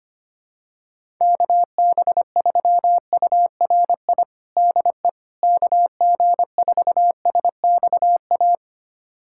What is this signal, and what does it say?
Morse code, heard as a keyed tone, about 25 wpm, 700 Hz. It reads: KB3URI DE KG4SXA
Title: USA Callsign DE Callsign 25wpm